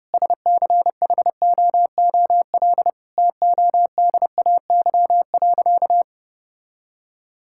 Morse code, heard as a keyed tone, about 30 wpm, 700 Hz. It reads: SCHOOL TODAY.
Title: Sentences from Top 1000 Words - Review Part 5 - 30wpm